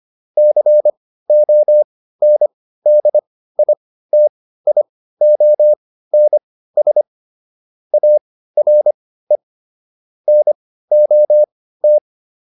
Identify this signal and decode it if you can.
Morse code, heard as a keyed tone, about 25 wpm, spaced as for 15 wpm, 600 Hz. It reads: CONDITIONS ARE NOT
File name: QSO407 15